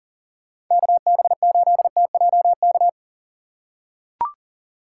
Morse code, heard as a keyed tone, about 40 wpm, 700 Hz. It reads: KB8TJK
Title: USA Call Signs - 2x Character Spacing - 40wpm